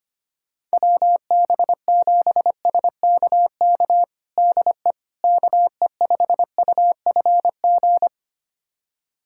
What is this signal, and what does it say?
Morse code, heard as a keyed tone, about 25 wpm, 700 Hz. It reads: WB7SKK DE KE5UFG
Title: USA Callsign DE Callsign - 6 Characters - 25wpm